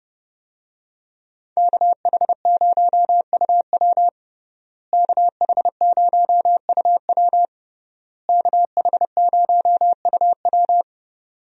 Morse code, heard as a keyed tone, about 30 wpm, 700 Hz. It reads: KH0UW KH0UW KH0UW
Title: USA Call Signs - 5 Characters - Repeated 3x - 30wpm